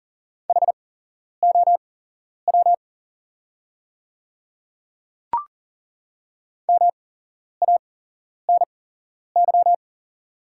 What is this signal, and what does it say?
Morse code, heard as a keyed tone, about 40 wpm, spaced as for 11 wpm, 690 Hz. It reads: HOW MANY
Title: 8x Character Spacing - Top 100 Words - 40wpm